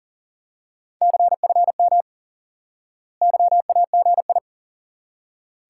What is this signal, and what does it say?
Morse code, heard as a keyed tone, about 40 wpm, 700 Hz. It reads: CFM YAGI
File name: IQE Course 3 - Lesson 89 - YAGI - Two Elements - 40wpm